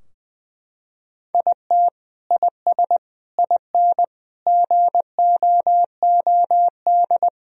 Morse code, heard as a keyed tone, about 20 wpm, 700 Hz. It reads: IT IS IN GOOD
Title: Sentences from Top 1000 Words - Review Part 4 - 20wpm